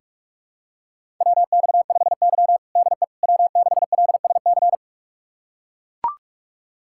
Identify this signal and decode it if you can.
Morse code, heard as a keyed tone, about 41 wpm, 700 Hz. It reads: WX5Y DE W6LSC
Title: USA Callsign DE Callsign 45wpm